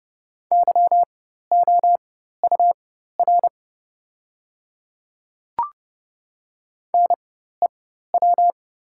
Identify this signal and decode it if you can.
Morse code, heard as a keyed tone, about 30 wpm, spaced as for 14 wpm, 700 Hz. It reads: YOUR NEW
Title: 4x Character Spacing - Top 500 Words - 30wpm